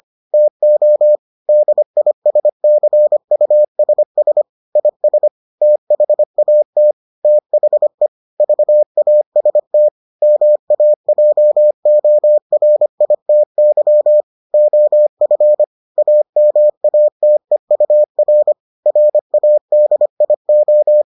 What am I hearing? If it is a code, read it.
Morse code, heard as a keyed tone, about 25 wpm, 600 Hz. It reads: TO DISCUSS IS THAT THE VAST MAJORITY OF AMATEUR RADIO